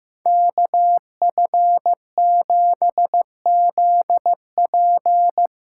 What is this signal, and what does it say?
Morse code, heard as a keyed tone, about 15 wpm, 700 Hz. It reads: KF7ZP
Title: USA Call Signs - 6 Characters - 8x Character Spacing - 15wpm